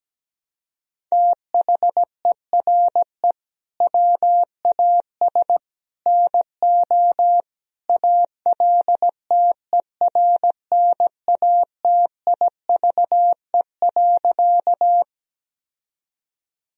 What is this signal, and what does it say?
Morse code, heard as a keyed tone, about 17 wpm, 700 Hz. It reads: THERE WAS NO ALTERNATIVE.
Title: Sentences from Top 1000 Words - Review Part 4 - 17wpm